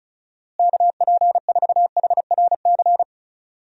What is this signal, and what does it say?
Morse code, heard as a keyed tone, about 36 wpm, 700 Hz. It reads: KP4HRC
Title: USA Call Signs - 6 Characters - Repeated 2x - 35wpm